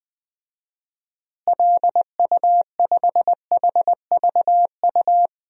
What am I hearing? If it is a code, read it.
Morse code, heard as a keyed tone, about 20 wpm, 700 Hz. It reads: LU5HVU
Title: DXCC-100 Call Signs - 6 Characters - Repeated 2x - 20wpm